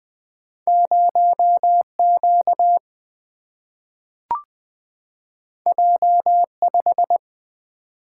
Morse code, heard as a keyed tone, about 20 wpm, 700 Hz. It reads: 0Q J5
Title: Single-Letter Single-Number 20wpm